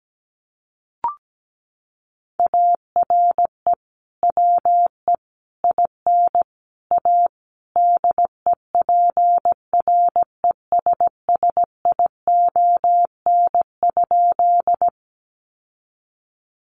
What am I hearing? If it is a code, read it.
Morse code, heard as a keyed tone, about 17 wpm, 700 Hz. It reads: ARE WE IN A DEPRESSION?
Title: Sentences from Top 1300 Words 17wpm